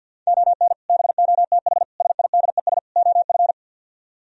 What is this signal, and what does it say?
Morse code, heard as a keyed tone, about 63 wpm, 700 Hz. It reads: ON BOTH SIDES OF